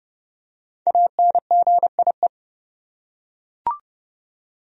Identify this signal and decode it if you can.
Morse code, heard as a keyed tone, about 30 wpm, 700 Hz. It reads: ANGIE
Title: USA Common Names - Max 5 Letters - 30wpm